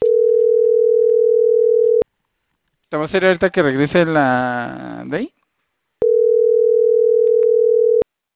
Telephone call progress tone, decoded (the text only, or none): ringback tone